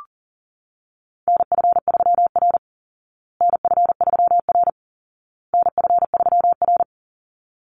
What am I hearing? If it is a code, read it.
Morse code, heard as a keyed tone, about 40 wpm, 700 Hz. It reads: NF3R NF3R NF3R
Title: Call Signs Repeated 2x - 40wpm